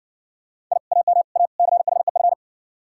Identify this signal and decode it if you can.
Morse code, heard as a keyed tone, about 66 wpm, 700 Hz. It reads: I AM A CHEF